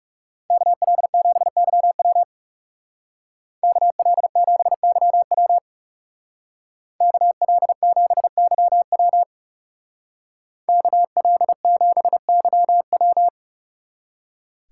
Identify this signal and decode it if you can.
Morse code, heard as a keyed tone, about 37 wpm, 700 Hz. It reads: KL7YW KL7YW KL7YW KL7YW
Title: Speed Racing - USA Call Signs 30wpm